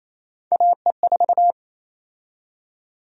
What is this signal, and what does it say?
Morse code, heard as a keyed tone, about 28 wpm, 700 Hz. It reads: AE4
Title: USA Call Sign Prefixes - 28wpm